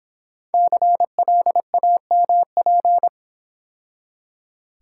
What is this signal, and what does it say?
Morse code, heard as a keyed tone, about 26 wpm, 700 Hz. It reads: CLAMP